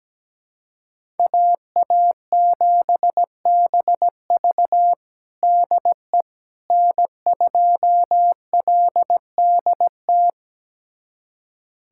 Morse code, heard as a keyed tone, about 17 wpm, 700 Hz. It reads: AA7BV DE N2LDT
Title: USA Callsign DE Callsign - 5 Characters - 17wpm